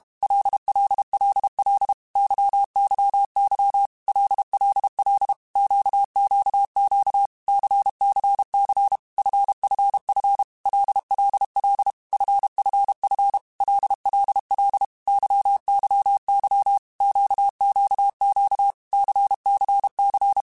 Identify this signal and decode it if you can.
Morse code, heard as a keyed tone, about 32 wpm, 800 Hz. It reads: LLLL YYY LLL QQQ CCC FFF LLL FFF LLL YYY QQQ CCC
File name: Groups@30 / Groups@30 mp3s / LFCYQ@30